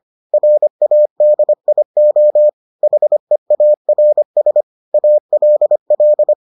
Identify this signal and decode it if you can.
Morse code, heard as a keyed tone, about 25 wpm, 600 Hz. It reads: RADIO HEARS ALL